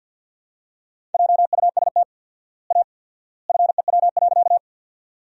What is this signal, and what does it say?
Morse code, heard as a keyed tone, about 51 wpm, 700 Hz. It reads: JUST A FEW.
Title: 4x Word-Spacing - Sentences from Top 300 Words - 50wpm